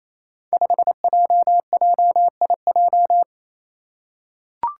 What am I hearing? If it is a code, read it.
Morse code, heard as a keyed tone, about 28 wpm, 700 Hz. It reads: 5JJIJ
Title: Five Challenging Letter-Number Combinations 28wpm